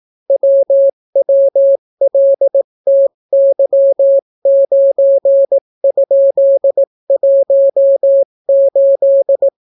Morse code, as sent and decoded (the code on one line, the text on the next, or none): .-- .-- .-.. - -.-- ----. ..--.. .---- ---..
WWLTY9?18